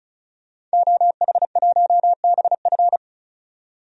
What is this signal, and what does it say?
Morse code, heard as a keyed tone, about 36 wpm, 700 Hz. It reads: OH1BF
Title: DXCC-100 Call Signs - 5 Characters - 35wpm